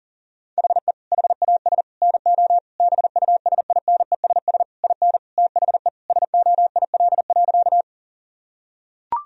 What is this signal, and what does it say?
Morse code, heard as a keyed tone, about 40 wpm, 700 Hz. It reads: HE HAS NO BUSINESS IN THE SOIL.
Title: Sentences from Top 1000 Words - Review Part 4 - 40wpm